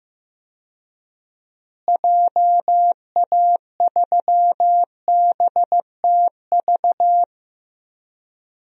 Morse code, heard as a keyed tone, about 15 wpm, 700 Hz. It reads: JA3BTV
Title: DXCC-100 Call Signs - 6 Characters - Repeated 3x - 15wpm